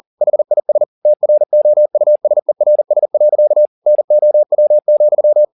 Morse code, heard as a keyed tone, about 40 wpm, 600 Hz. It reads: HIS TROUSERS. NOW,